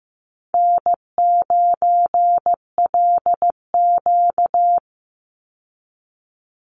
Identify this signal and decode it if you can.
Morse code, heard as a keyed tone, about 15 wpm, 700 Hz. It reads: N9LQ
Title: Call Signs Repeated 2x - 15wpm